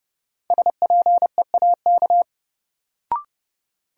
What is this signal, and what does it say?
Morse code, heard as a keyed tone, about 31 wpm, 700 Hz. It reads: SPEAK
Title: Sentences from Top 1000 Words - Review Part 2 - New Word Drill - 30wpm